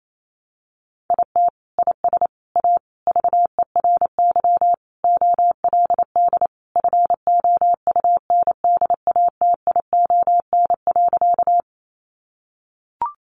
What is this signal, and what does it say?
Morse code, heard as a keyed tone, about 28 wpm, 700 Hz. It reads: IT IS A VERY OLD FOUNDATION.
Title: Sentences from Top 1200 Words 28wpm